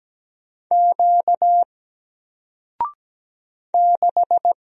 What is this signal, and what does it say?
Morse code, heard as a keyed tone, about 17 wpm, 700 Hz. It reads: Q 6
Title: Single Letter-Number 17wpm